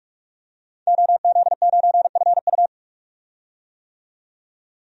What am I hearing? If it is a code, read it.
Morse code, heard as a keyed tone, about 46 wpm, 700 Hz. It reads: OZ9FU
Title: DXCC-100 Call Signs - 5 Characters - 45wpm